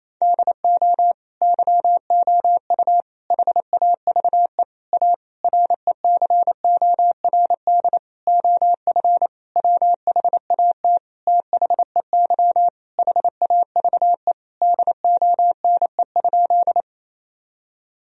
Morse code, heard as a keyed tone, about 28 wpm, 700 Hz. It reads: DO YOU HAVE A RECORD OF WHAT THEY HAVE DONE?
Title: Sentences from Top 1000 Words - Review Part 3 - 28wpm